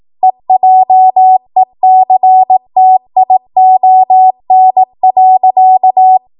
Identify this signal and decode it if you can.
Morse code, heard as a keyed tone, about 18 wpm, 750 Hz. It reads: EJECTION.
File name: Morse / 18WPM